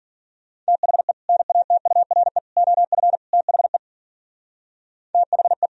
Morse code, heard as a keyed tone, about 50 wpm, 700 Hz. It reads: THE NATURE OF THE THE
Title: Speed-Racing - Sets of 4 Words 40wpm